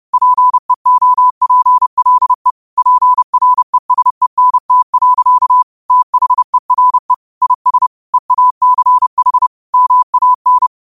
Morse code, heard as a keyed tone, about 30 wpm, 1000 Hz. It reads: PEOPLE PRESENT. THERE IS EACH MAN
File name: qotdfiles / 30wpm